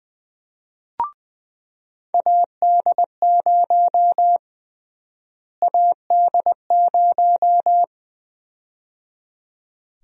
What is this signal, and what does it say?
Morse code, heard as a keyed tone, about 20 wpm, 700 Hz. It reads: AD0 AD0
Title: USA Call Sign Prefixes - Repeated 2x - 20wpm